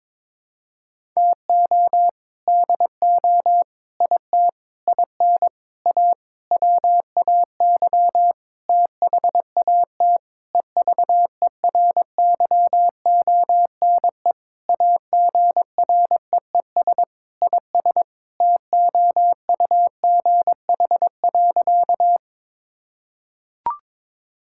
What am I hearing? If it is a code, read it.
Morse code, heard as a keyed tone, about 22 wpm, 700 Hz. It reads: TO DO IT IN A WAY THAT EVERYONE AGREES IS TOUGH.
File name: Sentences with 7-Word Phrases - 22wpm